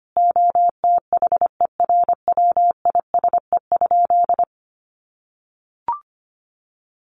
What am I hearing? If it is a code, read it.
Morse code, heard as a keyed tone, about 25 wpm, 700 Hz. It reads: OTHERWISE?
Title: Sentences from Top 1200 Words 25wpm